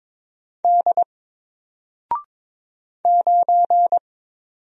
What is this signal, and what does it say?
Morse code, heard as a keyed tone, about 22 wpm, 700 Hz. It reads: D 9
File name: Single-Letter-Number-22wpm